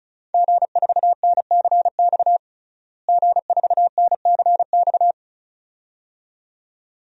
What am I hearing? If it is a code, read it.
Morse code, heard as a keyed tone, about 34 wpm, 700 Hz. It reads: G4NCX G4NCX
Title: DXCC-100 Call Signs - 5 Characters - Repeated 2x - 35wpm